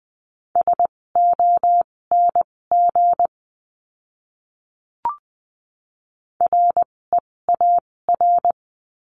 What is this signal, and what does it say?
Morse code, heard as a keyed tone, about 20 wpm, spaced as for 16 wpm, 700 Hz. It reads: SONG REAR
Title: 256 Lesson 42a - Words 12-20wpm